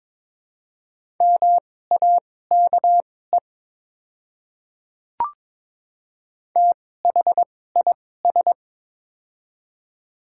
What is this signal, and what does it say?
Morse code, heard as a keyed tone, about 22 wpm, spaced as for 16 wpm, 700 Hz. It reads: MAKE THIS
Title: 2x Character Spacing - Top 500 Words - 22wpm